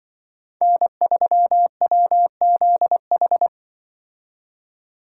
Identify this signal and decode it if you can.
Morse code, heard as a keyed tone, about 24 wpm, 700 Hz. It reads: N3WZH
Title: Speed Racing - USA 1x3 Call Signs 20wpm